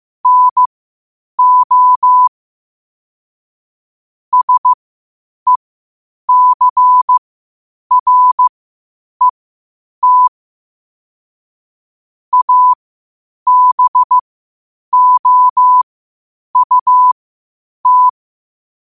Morse code, heard as a keyed tone, about 15 wpm, spaced as for 8 wpm, 1000 Hz. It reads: NO SECRET ABOUT
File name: qotdfiles / 8wpm